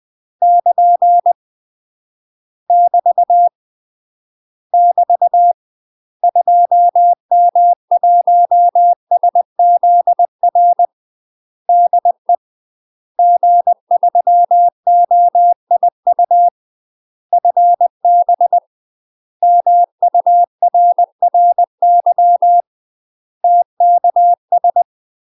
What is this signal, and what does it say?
Morse code, heard as a keyed tone, about 20 wpm, 700 Hz. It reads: <KN> <BT> <BT> 2M1SZR DE G3OIU FB MURRY TKS
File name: UK MASTER QSOs TRAINING / 20WPM / UK MASTER QSOs 0042